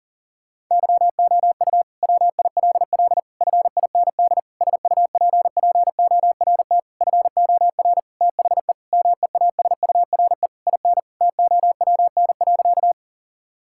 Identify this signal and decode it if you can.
Morse code, heard as a keyed tone, about 37 wpm, 700 Hz. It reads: YOU WILL FIND SUPPORT FOR THE MEASURE IN TOWN.
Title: Sentences from Top 1000 Words - Review Part 5 - 40wpm